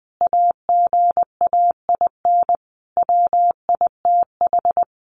Morse code, heard as a keyed tone, about 20 wpm, 700 Hz. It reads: AGAIN WITH
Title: Sentences from Top 1100 Words 20wpm